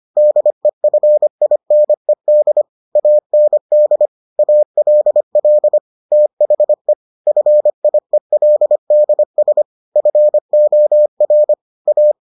Morse code, heard as a keyed tone, about 25 wpm, 600 Hz. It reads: DEFINED AND ALL THE FIELDS FOR A